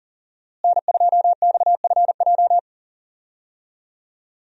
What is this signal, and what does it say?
Morse code, heard as a keyed tone, about 40 wpm, 700 Hz. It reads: N2XFJ
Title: USA Call Signs - 5 Characters - Repeated 3x - 40wpm